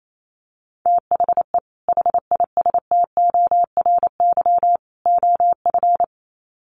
Morse code, heard as a keyed tone, about 28 wpm, 700 Hz. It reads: THE HISTORY OF
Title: Sets of 3 Words 28wpm